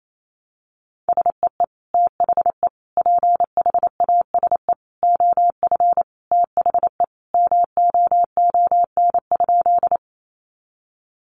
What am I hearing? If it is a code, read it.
Morse code, heard as a keyed tone, about 28 wpm, 700 Hz. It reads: SEE THE PHASE OF THE MOON?
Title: Sentences from Top 1200 Words 28wpm